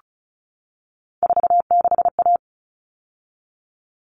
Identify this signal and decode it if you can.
Morse code, heard as a keyed tone, about 34 wpm, 700 Hz. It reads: 46A
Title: All Cut Numbers - 3-Digit 35wpm